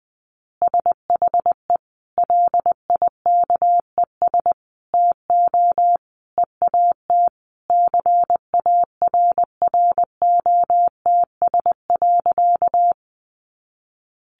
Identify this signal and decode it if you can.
Morse code, heard as a keyed tone, about 19 wpm, 700 Hz. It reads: SHE LIKES TO EAT CARROTS.
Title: Easy Phrases 20wpm